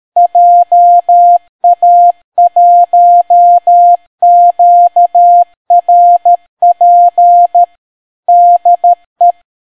Morse code, heard as a keyed tone, about 13 wpm, 700 Hz. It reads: JA1QRP DE